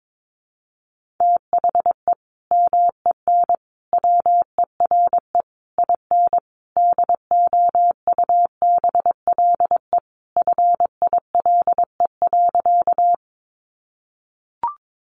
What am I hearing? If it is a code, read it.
Morse code, heard as a keyed tone, about 22 wpm, 700 Hz. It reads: THE MEN WERE IN DOUBLE FILE.
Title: Sentences from Top 1400 Words 22wpm